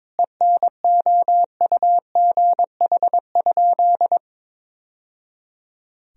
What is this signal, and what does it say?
Morse code, heard as a keyed tone, about 22 wpm, 700 Hz. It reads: ENOUGH?
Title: Sentences from Top 1000 Words - Review Part 2 - 22wpm